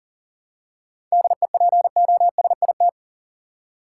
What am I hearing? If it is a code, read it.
Morse code, heard as a keyed tone, about 40 wpm, 700 Hz. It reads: DEPOSIT
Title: Seven Letter Words - 40wpm